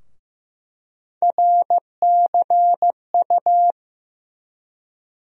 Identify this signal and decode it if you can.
Morse code, heard as a keyed tone, about 15 wpm, 700 Hz. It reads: RCU
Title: USA Call Sign Postfixes - 3 Letters 15wpm